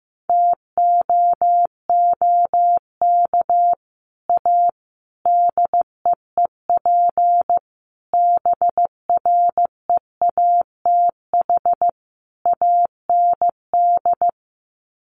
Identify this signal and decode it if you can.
Morse code, heard as a keyed tone, about 15 wpm, 700 Hz. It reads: TOOK A DEEP BREATH AND
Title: Sets of 5 Words Encore 15wpm